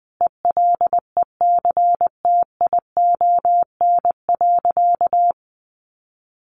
Morse code, heard as a keyed tone, about 20 wpm, 700 Hz. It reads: ELECTION.
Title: Sentences from Top 1500 Words 20wpm